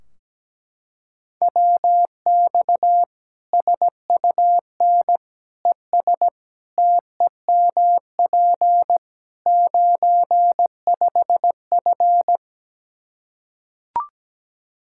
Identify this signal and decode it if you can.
Morse code, heard as a keyed tone, about 17 wpm, 700 Hz. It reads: WX SUN ES TEMP 95F
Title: K1USN SST QSO Practice 17wpm